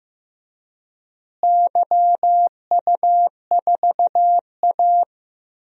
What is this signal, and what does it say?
Morse code, heard as a keyed tone, about 15 wpm, 700 Hz. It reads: YU4A
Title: DXCC-100 Call Signs - 4 Characters - Repeated 3x - 15wpm